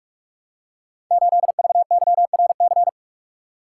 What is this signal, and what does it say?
Morse code, heard as a keyed tone, about 44 wpm, 700 Hz. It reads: 8VYRC
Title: Code Groups - 5 Letter:Numbers - 45wpm